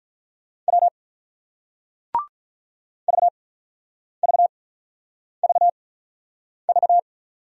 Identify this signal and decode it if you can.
Morse code, heard as a keyed tone, about 47 wpm, spaced as for 9 wpm, 700 Hz. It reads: W VVVV